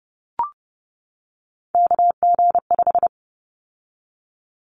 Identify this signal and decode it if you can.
Morse code, heard as a keyed tone, about 30 wpm, 700 Hz. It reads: KG5
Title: Callsign Prefixes for Top 25 DXCC - 30wpm